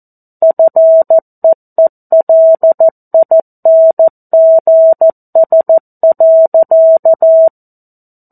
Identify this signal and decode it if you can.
Morse code, heard as a keyed tone, about 14 wpm, 650 Hz. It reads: FEELINGS.
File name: 02-Chapter-02 14 WPM